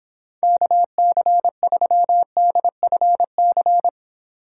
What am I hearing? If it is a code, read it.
Morse code, heard as a keyed tone, about 26 wpm, 700 Hz. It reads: KC3DFC